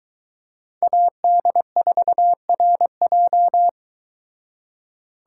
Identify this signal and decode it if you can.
Morse code, heard as a keyed tone, about 22 wpm, 700 Hz. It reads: AD4RJ